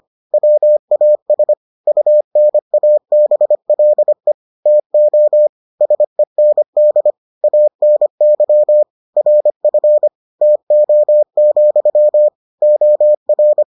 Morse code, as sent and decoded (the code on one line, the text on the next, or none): .-- .- ... / ..- -. .- -... .-.. / - --- / ... -. -.. / .- -. -.-- / .-. ..-. / - --- --..-- / --- .-.
WAS UNABLE TO SEND ANY RF TO, OR